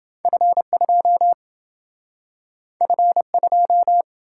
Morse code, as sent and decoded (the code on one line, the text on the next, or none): ..-. ..--- / ..-. ..---
F2 F2